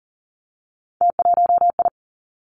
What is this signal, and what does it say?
Morse code, heard as a keyed tone, about 39 wpm, 700 Hz. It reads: T1I
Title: Three Challenging Letter-Number Combinations 40wpm